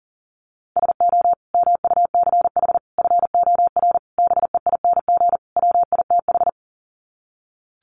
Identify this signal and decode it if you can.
Morse code, heard as a keyed tone, about 40 wpm, 700 Hz. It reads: SO MUCH FOR BEING WITH
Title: Sets of 5 Words Encore 40wpm